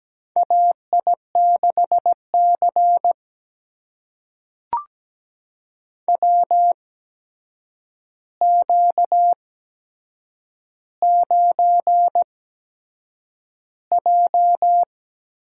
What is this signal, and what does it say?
Morse code, heard as a keyed tone, about 17 wpm, spaced as for 7 wpm, 700 Hz. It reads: AI6C WQ9J
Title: USA Call Signs - 4 Characters - 8x Character Spacing - 17wpm